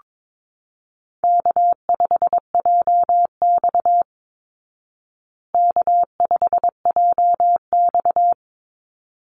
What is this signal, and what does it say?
Morse code, heard as a keyed tone, about 22 wpm, 690 Hz. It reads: K5JX K5JX
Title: Easy Call Signs Repeated 2x - 22wpm